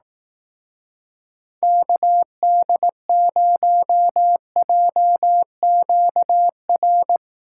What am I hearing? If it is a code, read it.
Morse code, heard as a keyed tone, about 18 wpm, 700 Hz. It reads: KD0JQR
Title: Speed Racing - USA 2x3 Call Signs 15wpm